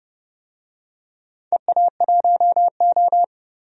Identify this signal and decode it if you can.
Morse code, heard as a keyed tone, about 30 wpm, 700 Hz. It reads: EA1O
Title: DXCC-100 Call Signs - 4 Characters - 30wpm